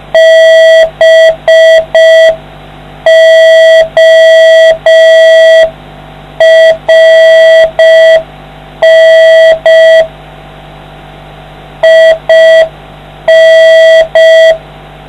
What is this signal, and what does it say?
Morse code, heard as a keyed tone, about 5 wpm, 650 Hz, with noise. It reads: BORN IN